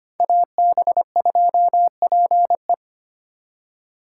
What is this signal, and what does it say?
Morse code, heard as a keyed tone, about 31 wpm, 700 Hz. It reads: AB2PE
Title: Speed Racing - USA Call Signs 25wpm